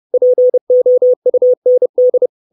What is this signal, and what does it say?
Morse code, heard as a keyed tone, about 30 wpm, 500 Hz. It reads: POUND
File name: 632 Words 30WPM